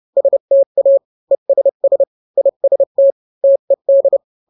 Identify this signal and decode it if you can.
Morse code, heard as a keyed tone, about 30 wpm, 550 Hz. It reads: STA ESS IST TED